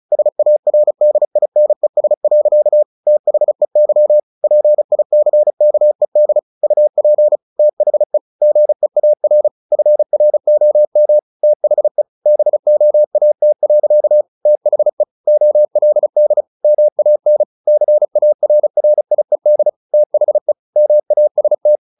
Morse code, heard as a keyed tone, about 34 wpm, 600 Hz. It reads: SARDINES. THEY PICKED UP THE GEAR FROM THE BOAT. THE OLD MAN CARRIED THE MAST
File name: the old man and the sea / the old man and the sea 1 35wpm